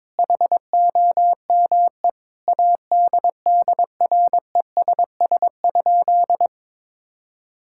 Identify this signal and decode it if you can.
Morse code, heard as a keyed tone, about 22 wpm, 700 Hz. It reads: HOME ADDRESS?
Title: Sentences from Top 1000 Words - Review Part 6 - 22wpm